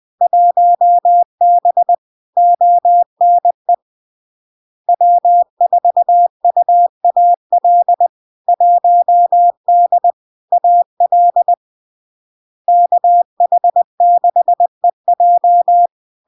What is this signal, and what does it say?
Morse code, heard as a keyed tone, about 20 wpm, 700 Hz. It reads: 1B ONE W4UAL 1D AL KH6EJ
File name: LARK-FD-2022-Log-in-CW-20wpm-700-Hz